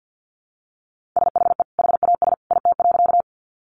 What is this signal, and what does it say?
Morse code, heard as a keyed tone, about 50 wpm, 700 Hz. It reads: SHE HAS IT.
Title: Sentences from Top 100 Words 50wpm